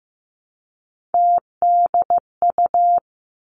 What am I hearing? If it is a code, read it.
Morse code, heard as a keyed tone, about 15 wpm, 700 Hz. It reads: TDU
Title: Three Challenging Letter-Number Combinations 15wpm